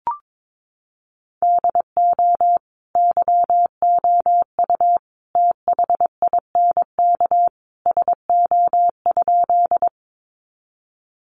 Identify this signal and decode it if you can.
Morse code, heard as a keyed tone, about 22 wpm, 700 Hz. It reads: DO YOU THINK SO?
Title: Sentences from Top 200 Words 22wpm